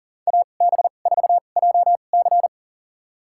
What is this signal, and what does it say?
Morse code, heard as a keyed tone, about 40 wpm, spaced as for 29 wpm, 700 Hz. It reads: AB4JC
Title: USA Call Signs - 5 Characters - 2x Character Spacing - 40wpm